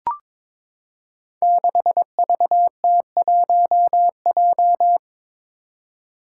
Morse code, heard as a keyed tone, about 21 wpm, 700 Hz. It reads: 6VT1J
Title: Five Challenging Letter-Number Combinations 22wpm